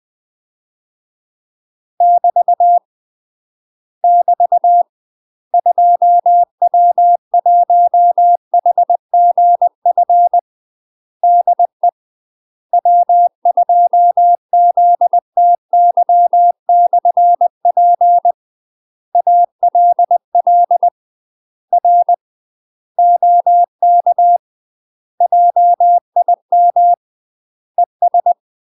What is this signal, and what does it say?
Morse code, heard as a keyed tone, about 20 wpm, 700 Hz. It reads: <BT> <BT> 2W1HGF DE W2ZTY/P ALL R OK JIM ES
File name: UK MASTER QSOs TRAINING / 20WPM / UK MASTER QSOs 0004